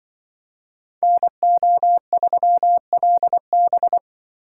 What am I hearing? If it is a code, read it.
Morse code, heard as a keyed tone, about 24 wpm, 700 Hz. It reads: NO3LB
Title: Speed Racing - USA Call Signs 20wpm